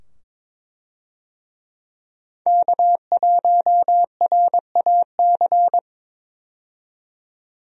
Morse code, heard as a keyed tone, about 21 wpm, 700 Hz. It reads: K1RAC
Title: USA Call Signs - 5 Characters - 22wpm